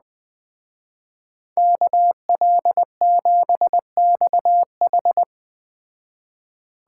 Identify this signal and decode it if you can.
Morse code, heard as a keyed tone, about 20 wpm, 700 Hz. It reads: KL7XH
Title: Speed Racing - USA 2x2 Call Signs 17wpm